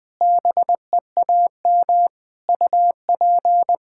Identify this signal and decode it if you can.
Morse code, heard as a keyed tone, about 20 wpm, 700 Hz. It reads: BEAM UP
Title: Practice QSO - Exchange 2 - Info - 20wpm